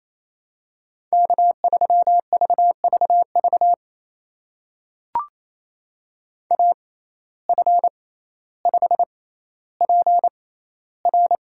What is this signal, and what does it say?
Morse code, heard as a keyed tone, about 28 wpm, spaced as for 14 wpm, 700 Hz. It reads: K3VVV AF5PR
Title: USA Call Signs - 5 Characters - 6x Character Spacing - 28wpm